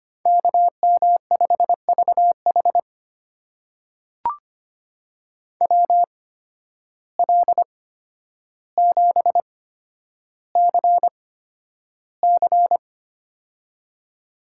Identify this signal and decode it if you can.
Morse code, heard as a keyed tone, about 25 wpm, spaced as for 11 wpm, 700 Hz. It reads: KM5VH WL7CC
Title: USA Call Signs - 5 Characters - 8x Character Spacing - 25wpm